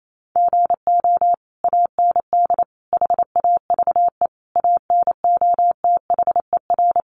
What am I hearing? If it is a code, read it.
Morse code, heard as a keyed tone, about 28 wpm, 700 Hz. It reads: GO AND HAVE ANOTHER
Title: Sentences from Top 600 Words 28wpm